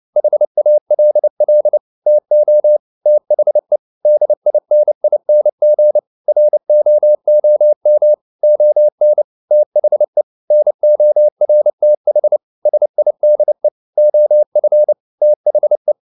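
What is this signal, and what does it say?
Morse code, heard as a keyed tone, about 29 wpm, 600 Hz. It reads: HALL TO THE DINING ROOM ON THE NORTH SIDE OF THE
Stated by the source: Morse code at 29 wpm, sending Ken Follett's novel Never in English, from CW med Gnister.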